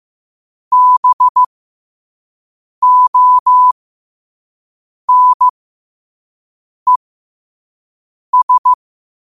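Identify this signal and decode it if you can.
Morse code, heard as a keyed tone, about 15 wpm, spaced as for 5 wpm, 1000 Hz. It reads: BONES
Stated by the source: Morse code practice at 5 wpm.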